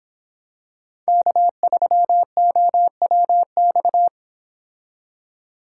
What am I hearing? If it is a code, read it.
Morse code, heard as a keyed tone, about 27 wpm, 700 Hz. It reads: K3OWX
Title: Speed Racing - USA 1x3 Call Signs 22wpm